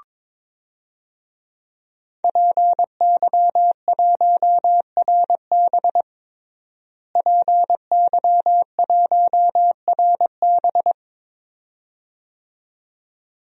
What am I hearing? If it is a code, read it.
Morse code, heard as a keyed tone, about 22 wpm, 700 Hz. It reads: PY1RB PY1RB
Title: DXCC-100 Call Signs - 5 Characters - Repeated 2x - 22wpm